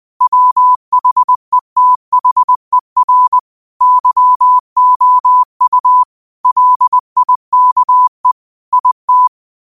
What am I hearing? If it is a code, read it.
Morse code, heard as a keyed tone, about 20 wpm, 1000 Hz. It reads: WHETHER YOU LIKE IT
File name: qotdfiles / 20wpm